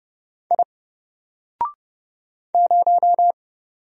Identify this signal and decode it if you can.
Morse code, heard as a keyed tone, about 30 wpm, 700 Hz. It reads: I 0
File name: Single-Letter-Number-30wpm